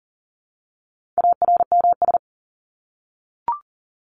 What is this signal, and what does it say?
Morse code, heard as a keyed tone, about 40 wpm, 700 Hz. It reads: ARMS
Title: ES:S Suffix - Top 100 Words with Base Words 40wpm